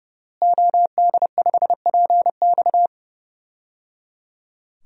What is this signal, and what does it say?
Morse code, heard as a keyed tone, about 30 wpm, 700 Hz. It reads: OD5PX
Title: DXCC-100 Call Signs - 5 Characters - Repeated 3x - 30wpm